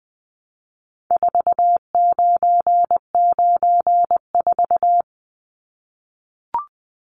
Morse code, heard as a keyed tone, about 20 wpm, 700 Hz. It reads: 4994
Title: Numbers - 4-Digit 20wpm